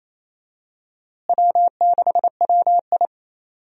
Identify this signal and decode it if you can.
Morse code, heard as a keyed tone, about 28 wpm, 700 Hz. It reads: W6WI
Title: USA Call Signs - 4 Characters - Repeated 2x - 28wpm